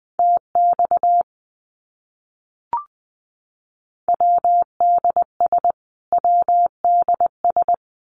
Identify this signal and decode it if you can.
Morse code, heard as a keyed tone, about 20 wpm, 700 Hz. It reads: TX WDS WDS
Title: Abbreviations Repeated 3x - 20wpm